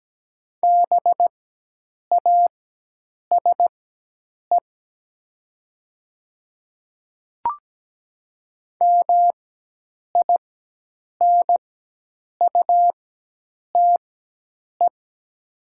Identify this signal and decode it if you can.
Morse code, heard as a keyed tone, about 17 wpm, spaced as for 8 wpm, 700 Hz. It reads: BASE MINUTE
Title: 4x Character Spacing - Top 500 Words - 17wpm